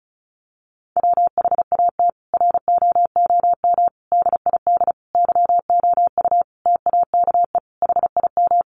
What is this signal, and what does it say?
Morse code, heard as a keyed tone, about 34 wpm, 700 Hz. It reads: WHAT ROOM DID YOU TAKE HIM
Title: Sentences from Top 600 Words 35wpm